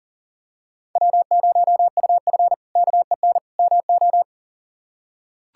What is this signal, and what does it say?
Morse code, heard as a keyed tone, about 37 wpm, 700 Hz. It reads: W0UF KEN MO
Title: K1USN SST QSO Practice - 40wpm